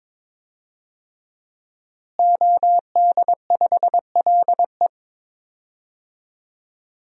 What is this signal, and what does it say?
Morse code, heard as a keyed tone, about 22 wpm, 700 Hz. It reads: OD5LE
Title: DXCC-100 Call Signs - 5 Characters - Repeated 2x - 22wpm